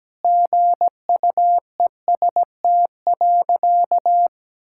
Morse code, heard as a keyed tone, about 17 wpm, 700 Hz. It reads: GUEST.
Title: Sentences from Top 1600 Words 17wpm